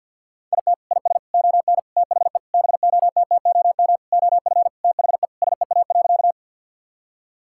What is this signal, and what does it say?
Morse code, heard as a keyed tone, about 50 wpm, 700 Hz. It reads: IT IS ON THE BOTTOM OF THE SEA.
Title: Sentences from Top 1000 Words - Review Part 6 - 50wpm